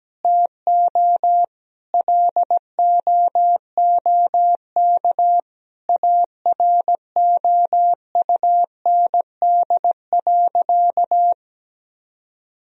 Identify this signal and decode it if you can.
Morse code, heard as a keyed tone, about 17 wpm, 700 Hz. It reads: TO LOOK AROUND.
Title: Sentences from Top 1000 Words - Review Part 3 - 17wpm